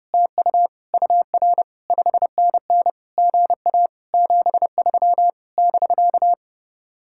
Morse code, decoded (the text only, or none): TU UR 5NN GA 73 <BK>